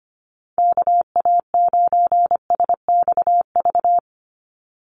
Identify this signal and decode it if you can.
Morse code, heard as a keyed tone, about 25 wpm, 700 Hz. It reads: KA9SXV
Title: Call Signs - 25wpm